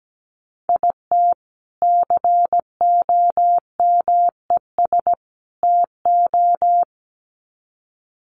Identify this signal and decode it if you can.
Morse code, heard as a keyed tone, about 17 wpm, 700 Hz. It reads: IT COMES TO
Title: Sets of 3 Words 17wpm